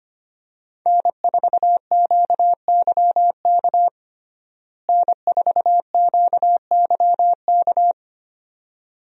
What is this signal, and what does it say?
Morse code, heard as a keyed tone, about 25 wpm, 700 Hz. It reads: N4QYK N4QYK
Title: USA Call Signs - 5 Characters - Repeated 2x - 25wpm